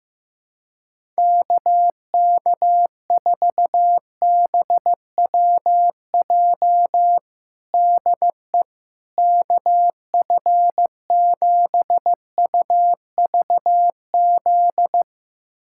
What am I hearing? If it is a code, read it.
Morse code, heard as a keyed tone, about 15 wpm, 700 Hz. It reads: KK4BWJ DE KF7UVZ
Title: USA Callsign DE Callsign - 6 Characters - 15wpm